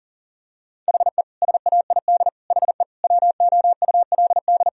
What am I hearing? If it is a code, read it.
Morse code, heard as a keyed tone, about 40 wpm, 700 Hz. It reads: HE SAID HE WOULD
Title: Sentences from Top 1000 Words - Review Part 4 - 40wpm